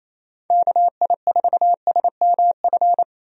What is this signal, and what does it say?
Morse code, heard as a keyed tone, about 28 wpm, 700 Hz. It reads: KI4SMF
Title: USA Call Signs - 6 Characters - Repeated 3x - 28wpm